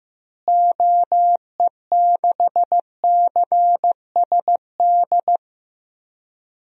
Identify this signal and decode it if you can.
Morse code, heard as a keyed tone, about 15 wpm, 700 Hz. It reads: OE6CSD
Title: DXCC-100 Call Signs - 6 Characters - Repeated 2x - 15wpm